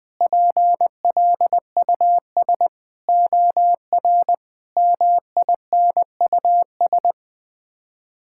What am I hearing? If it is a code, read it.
Morse code, heard as a keyed tone, about 20 wpm, 700 Hz. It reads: PLUS OR MINUS
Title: Sets of 8 Words - 20wpm